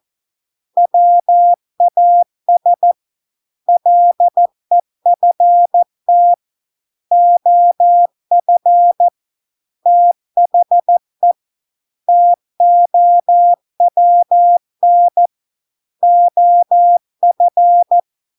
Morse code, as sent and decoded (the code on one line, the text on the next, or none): .-- .- ... / .-.. ..-. - / --- ..-. / - .... / - --- .-- -. / --- ..-.
WAS LEFT OF THE TOWN OF